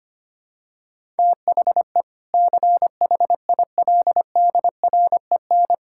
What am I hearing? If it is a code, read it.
Morse code, heard as a keyed tone, about 25 wpm, 700 Hz. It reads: THE CHILDREN
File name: practice / 25wpm / Sentences with 2-Word Phrases - 25wpm